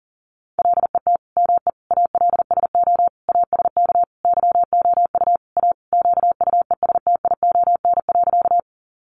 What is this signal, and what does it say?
Morse code, heard as a keyed tone, about 40 wpm, 700 Hz. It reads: LET ME ALSO ASK YOU A QUESTION.
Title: Sentences from Top 400 Words 40wpm